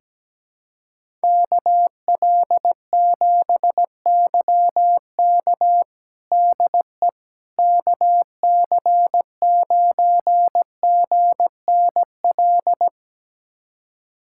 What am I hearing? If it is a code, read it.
Morse code, heard as a keyed tone, about 17 wpm, 700 Hz. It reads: KL7YK DE KC9GNL
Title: USA Callsign DE Callsign 17wpm